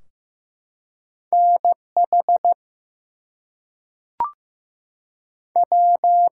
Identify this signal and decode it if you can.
Morse code, heard as a keyed tone, about 15 wpm, 700 Hz. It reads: NH W
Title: US States and Canadian Provinces 15wpm